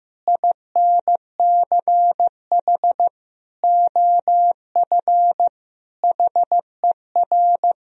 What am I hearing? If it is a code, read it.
Morse code, heard as a keyed tone, about 15 wpm, 700 Hz. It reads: INCH OF HER
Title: Sentences from Top 1000 Words - Review Part 3 - 15wpm